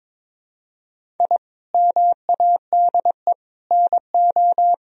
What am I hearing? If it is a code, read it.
Morse code, heard as a keyed tone, about 22 wpm, 700 Hz. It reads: I MADE NO
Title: Sentences from Top 1000 Words - Review Part 1 - 22wpm